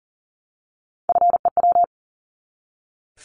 Morse code, heard as a keyed tone, about 40 wpm, 700 Hz. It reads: FEW
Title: Top 300 Words 40wpm